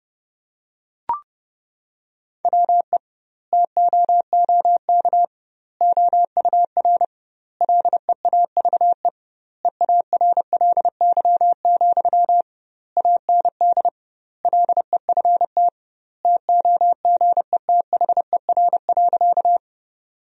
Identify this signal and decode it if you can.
Morse code, heard as a keyed tone, about 32 wpm, 700 Hz. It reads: WE TOOK OUR LEAVE EARLY, AND LEFT TOGETHER.
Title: 2x Word-Spacing - Sentences from Top 400 Words - 30wpm